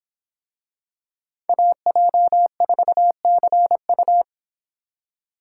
Morse code, heard as a keyed tone, about 25 wpm, 700 Hz. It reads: AJ4CU